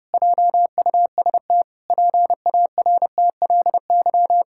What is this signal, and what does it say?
Morse code, heard as a keyed tone, about 30 wpm, 700 Hz. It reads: JUST PARTLY
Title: Pangram Challenge v2 - 30wpm